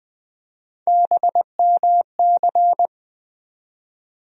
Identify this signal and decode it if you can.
Morse code, heard as a keyed tone, about 21 wpm, 700 Hz. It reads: BMC
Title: USA Call Sign Postfixes - 3 Letters 20wpm